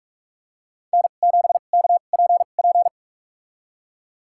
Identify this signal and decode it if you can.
Morse code, heard as a keyed tone, about 46 wpm, spaced as for 33 wpm, 700 Hz. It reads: N7KPP
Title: USA Call Signs - 5 Characters - 2x Character Spacing - 45wpm